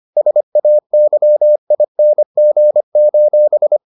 Morse code, heard as a keyed tone, about 25 wpm, 600 Hz. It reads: SAYING: